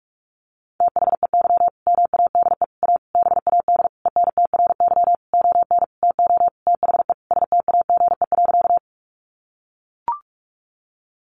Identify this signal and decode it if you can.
Morse code, heard as a keyed tone, about 44 wpm, 700 Hz. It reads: THEY MADE A BAD ENTRY ON TO THE STAGE.
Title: Sentences from Top 1500 Words 45wpm